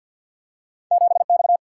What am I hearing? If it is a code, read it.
Morse code, heard as a keyed tone, about 52 wpm, 700 Hz. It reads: 7X
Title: Single-Letter Single-Number 50wpm